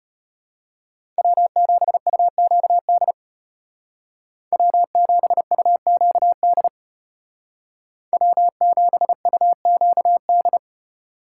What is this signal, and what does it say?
Morse code, heard as a keyed tone, about 33 wpm, 700 Hz. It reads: W7UQD W7UQD W7UQD